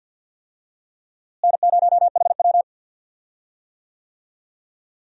Morse code, heard as a keyed tone, about 51 wpm, 700 Hz. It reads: N0HW
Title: USA Call Signs - 4 Characters - Repeated 2x - 50wpm